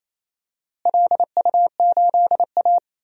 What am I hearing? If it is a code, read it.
Morse code, heard as a keyed tone, about 28 wpm, 700 Hz. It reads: LU8A
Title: DXCC-100 Call Signs - 4 Characters - Repeated 3x - 28wpm